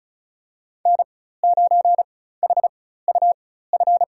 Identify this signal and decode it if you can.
Morse code, heard as a keyed tone, about 34 wpm, spaced as for 16 wpm, 700 Hz. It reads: N9HUF
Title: USA Call Signs - 5 Characters - 4x Character Spacing - 35wpm